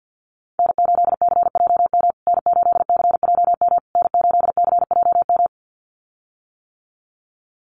Morse code, heard as a keyed tone, about 50 wpm, 700 Hz. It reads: N8CJM N8CJM N8CJM
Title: USA Call Signs Repeated 3x 50wpm